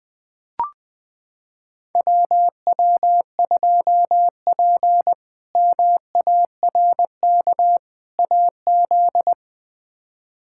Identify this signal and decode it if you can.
Morse code, heard as a keyed tone, about 20 wpm, 700 Hz. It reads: WW2P MARK AZ
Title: K1USN SST QSO Practice - 20wpm